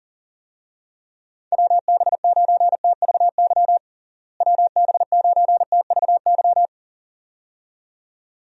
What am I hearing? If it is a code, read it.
Morse code, heard as a keyed tone, about 40 wpm, 700 Hz. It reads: WB9TVY WB9TVY
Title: USA Call Signs - 6 Characters - Repeated 2x - 40wpm